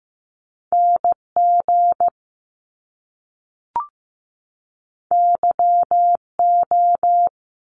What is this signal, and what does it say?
Morse code, heard as a keyed tone, about 15 wpm, 700 Hz. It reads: NG YO
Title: Top Two Letter Combinations in Words 15wpm